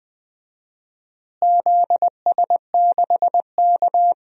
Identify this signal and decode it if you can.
Morse code, heard as a keyed tone, about 20 wpm, 700 Hz. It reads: ZS6K
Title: DXCC-100 Call Signs - 4 Characters - Repeated 2x - 20wpm